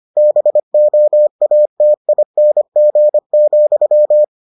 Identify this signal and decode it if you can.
Morse code, heard as a keyed tone, about 25 wpm, 600 Hz. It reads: BOATING,